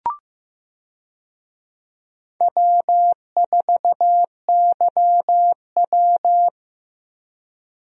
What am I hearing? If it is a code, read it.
Morse code, heard as a keyed tone, about 15 wpm, 700 Hz. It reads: W4YW
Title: USA Call Signs - 4 Characters - Repeated 2x - 15wpm